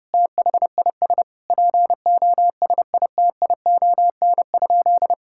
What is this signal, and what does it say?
Morse code, heard as a keyed tone, about 32 wpm, 700 Hz. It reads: THIS POSITION?
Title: Sentences from Top 1000 Words - Review Part 5 - 30wpm